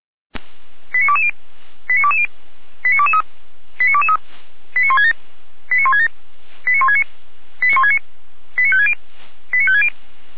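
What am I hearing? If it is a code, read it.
Five-tone selective call (ZVEI), heard as quick runs of five tones, tones of about 70 ms. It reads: 89220 89220 89223 89223 89167 89167 89169 89169 89560 89560
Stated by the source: Received on an SDR.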